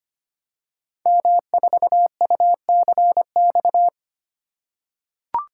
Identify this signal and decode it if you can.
Morse code, heard as a keyed tone, about 25 wpm, 700 Hz. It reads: M4UCX
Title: Code Groups - 5 Letter:Numbers - 25wpm